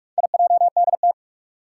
Morse code, heard as a keyed tone, about 46 wpm, 700 Hz. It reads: IJDT
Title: Four Challenging Letter-Number Combinations 45wpm